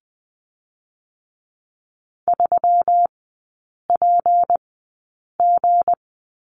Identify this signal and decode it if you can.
Morse code, heard as a keyed tone, about 19 wpm, spaced as for 8 wpm, 700 Hz. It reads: 3PG
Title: 166 Lesson 27e - Number 7 - Three Characters 20wpm